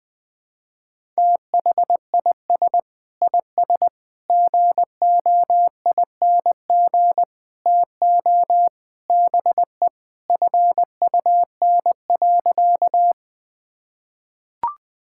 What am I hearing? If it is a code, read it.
Morse code, heard as a keyed tone, about 21 wpm, 700 Hz. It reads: THIS IS GOING TO BE FUN.
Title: Sentences with 4-Word Phrases - 20wpm